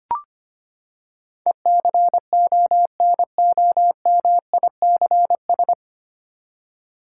Morse code, heard as a keyed tone, about 25 wpm, 700 Hz. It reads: ECONOMICS
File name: Nine Letter Words - 25wpm